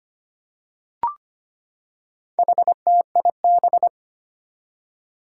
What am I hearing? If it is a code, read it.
Morse code, heard as a keyed tone, about 25 wpm, 700 Hz. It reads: HTIB
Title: Four Challenging Letter-Number Combinations 25wpm